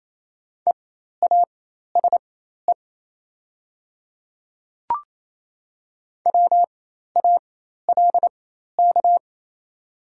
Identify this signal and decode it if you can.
Morse code, heard as a keyed tone, about 28 wpm, spaced as for 13 wpm, 700 Hz. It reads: EASE WALK
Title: 4x Character Spacing - Top 300 Words - 28wpm